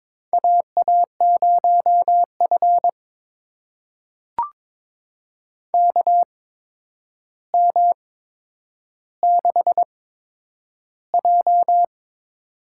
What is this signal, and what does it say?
Morse code, heard as a keyed tone, about 22 wpm, spaced as for 9 wpm, 700 Hz. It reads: AA0F KM6J
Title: USA Call Signs - 4 Characters - 8x Character Spacing - 22wpm